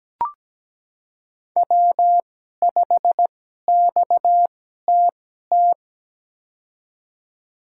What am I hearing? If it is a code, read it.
Morse code, heard as a keyed tone, about 17 wpm, spaced as for 12 wpm, 700 Hz. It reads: W5XTT
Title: USA Call Signs - 5 Characters - 2x Character Spacing - 17wpm